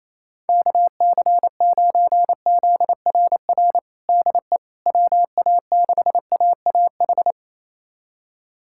Morse code, heard as a keyed tone, about 28 wpm, 700 Hz. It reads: KC9ZRR DE WA6AAH
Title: USA Callsign DE Callsign 28wpm